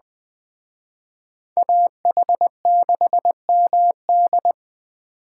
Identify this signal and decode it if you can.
Morse code, heard as a keyed tone, about 20 wpm, 700 Hz. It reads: AH6MD
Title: Speed Racing - USA 2x2 Call Signs 17wpm